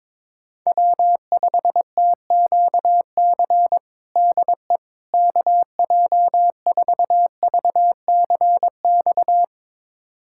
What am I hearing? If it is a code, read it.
Morse code, heard as a keyed tone, about 22 wpm, 700 Hz. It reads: W5TQC DE KJ4VCX
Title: USA Callsign DE Callsign 22wpm